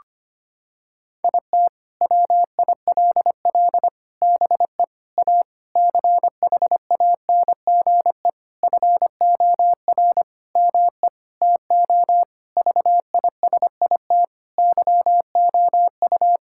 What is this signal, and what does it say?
Morse code, heard as a keyed tone, about 25 wpm, 700 Hz. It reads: IT WILL BE A CHANGE FOR ME TO VISIT YOU
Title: Sentences from Top 1000 Words - Review Part 6 - 25wpm